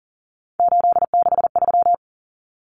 Morse code, heard as a keyed tone, about 40 wpm, 700 Hz. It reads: 863
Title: Common Cut Numbers - 3-Digit 40wpm